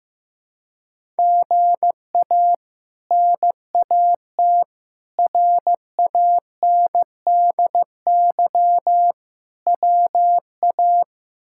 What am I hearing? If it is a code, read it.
Morse code, heard as a keyed tone, about 15 wpm, 700 Hz. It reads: GA NAT RANDY WA
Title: K1USN SST QSO Practice - 15wpm